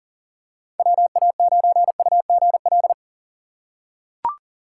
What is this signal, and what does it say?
Morse code, heard as a keyed tone, about 38 wpm, 700 Hz. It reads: WA9UGL
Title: USA Call Signs - 4x Character Spacing - 40wpm